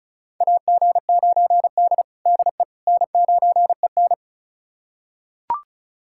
Morse code, heard as a keyed tone, about 36 wpm, 700 Hz. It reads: AG9D DE N9EN
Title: USA Callsign DE Callsign - 4 Characters - 35wpm